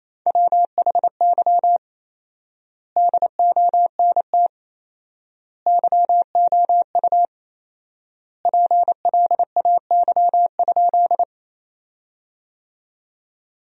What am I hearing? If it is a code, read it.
Morse code, heard as a keyed tone, about 27 wpm, 700 Hz. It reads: WHY DONT YOU PLAY?
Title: 4x Word-Spacing - Sentences from Top 300 Words - 28wpm